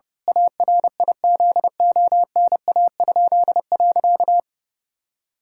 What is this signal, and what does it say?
Morse code, heard as a keyed tone, about 30 wpm, 700 Hz. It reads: ARIZONA?.
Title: Pangram Challenge v2 - 30wpm